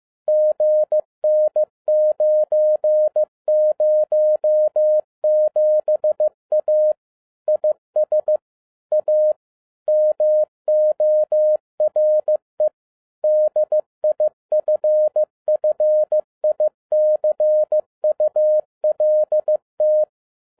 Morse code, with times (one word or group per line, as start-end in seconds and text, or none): 0.3-6.9 GN907A
7.5-8.4 IS
8.9-9.3 A
9.9-12.7 MORE
13.2-20.0 DIFFICULT